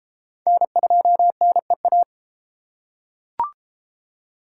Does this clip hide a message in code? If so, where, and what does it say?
Yes: Morse code from 0.5 to 2.0 s, heard as a keyed tone, about 30 wpm, 700 Hz. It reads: N2NEA